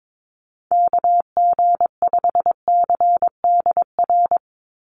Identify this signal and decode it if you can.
Morse code, heard as a keyed tone, about 22 wpm, 700 Hz. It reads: KG5CDR
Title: USA Call Signs 22wpm